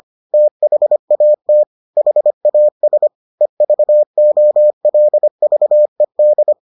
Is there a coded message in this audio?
Yes: Morse code from 0.3 to 6.5 s, heard as a keyed tone, about 25 wpm, 600 Hz. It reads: THAT HAS EVOLVED